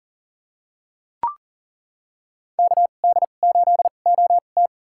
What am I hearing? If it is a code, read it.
Morse code, heard as a keyed tone, about 40 wpm, spaced as for 29 wpm, 700 Hz. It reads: KD8OT
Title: USA Call Signs - 2x Character Spacing - 40wpm